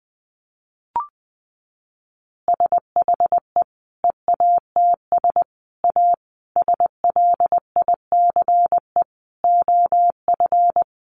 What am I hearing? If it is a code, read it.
Morse code, heard as a keyed tone, about 20 wpm, 700 Hz. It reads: SHE EATS A SLICE OF
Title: Easy Phrases 20wpm